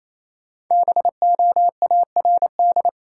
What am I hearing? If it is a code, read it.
Morse code, heard as a keyed tone, about 28 wpm, 700 Hz. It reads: BOARD
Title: Sentences from Top 700 Words - New Word Drill - 28wpm